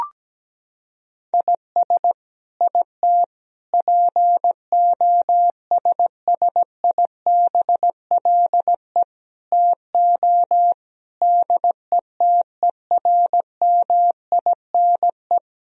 Morse code, heard as a keyed tone, about 17 wpm, 700 Hz. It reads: IS IT POSSIBLE TO DETERMINE
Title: Sentences from Top 1000 Words - Review Part 4 - 17wpm